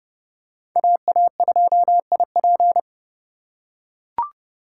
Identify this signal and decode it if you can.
Morse code, heard as a keyed tone, about 30 wpm, 700 Hz. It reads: AA2IP
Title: USA Call Signs - 5 Characters - 4x Character Spacing - 30wpm